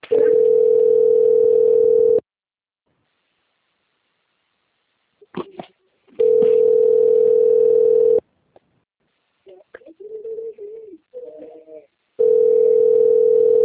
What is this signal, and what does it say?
Ringback tone (North America), 440 + 480 Hz.
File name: out- - - - - .17416